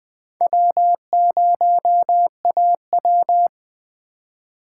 Morse code, heard as a keyed tone, about 20 wpm, 700 Hz. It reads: W0AW